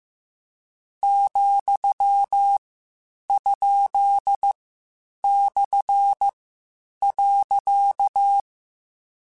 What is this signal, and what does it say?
Morse code, heard as a keyed tone, about 15 wpm, 800 Hz. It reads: , ? / .
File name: Misc / Misc-mp3s / Punct @13